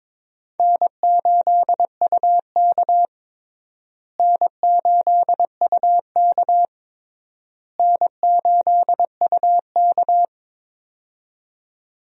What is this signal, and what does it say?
Morse code, heard as a keyed tone, about 22 wpm, 700 Hz. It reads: N8UK N8UK N8UK
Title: USA Call Signs - 4 Characters - Repeated 3x - 22wpm